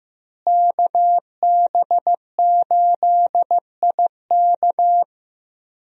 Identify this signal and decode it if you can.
Morse code, heard as a keyed tone, about 15 wpm, 700 Hz. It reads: KB8IK
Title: USA Call Signs - 5 Characters - Repeated 3x - 15wpm